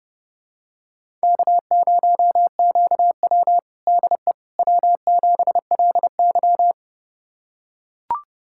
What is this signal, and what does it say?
Morse code, heard as a keyed tone, about 30 wpm, 700 Hz. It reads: K0QW DE W7LY
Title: USA Callsign DE Callsign - 4 Characters - 30wpm